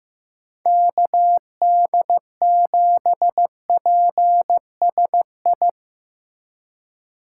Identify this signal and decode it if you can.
Morse code, heard as a keyed tone, about 15 wpm, 700 Hz. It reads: KD7PSI
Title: USA Call Signs - 6 Characters - 2x Character Spacing - 15wpm